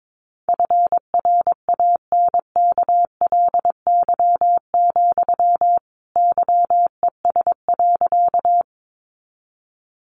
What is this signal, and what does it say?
Morse code, heard as a keyed tone, about 22 wpm, 700 Hz. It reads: FRANKLY, YES.
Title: A Sound Of Thunder 22wpm